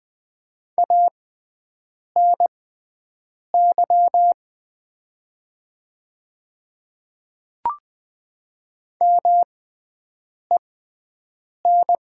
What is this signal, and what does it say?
Morse code, heard as a keyed tone, about 20 wpm, spaced as for 7 wpm, 700 Hz. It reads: ANY MEN
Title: 6x Character Spacing - Top 200 Words - 20wpm